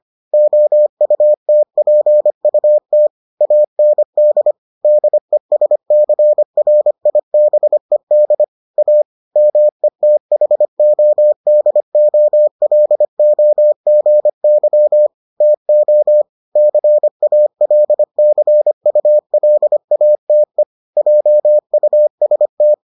Morse code, heard as a keyed tone, about 25 wpm, 600 Hz. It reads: OUTPUT AND DESCRIBED A METHODOLOGY TO CALCULATE JUST